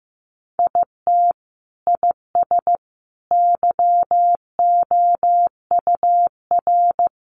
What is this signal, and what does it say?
Morse code, heard as a keyed tone, about 15 wpm, 700 Hz. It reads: IT IS YOUR
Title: Sentences from Top 1000 Words 15wpm